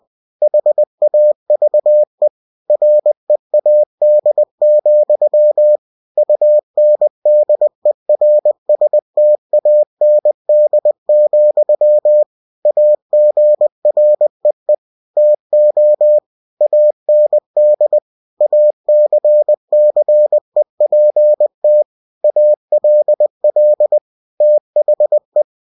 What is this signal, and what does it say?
Morse code, heard as a keyed tone, about 20 wpm, 600 Hz. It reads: HAVE READ, UNDERSTAND, AGREE TO AND ACCEPT ALL THE
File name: cw-books / winnie-the-pooh / winnie-20wpm-page-0113